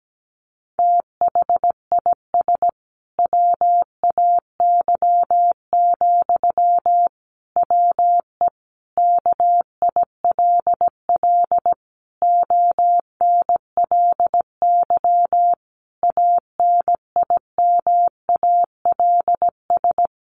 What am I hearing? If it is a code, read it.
Morse code, heard as a keyed tone, about 17 wpm, 700 Hz. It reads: THIS WAY, WE KILL ONLY ANIMALS
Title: A Sound Of Thunder 17wpm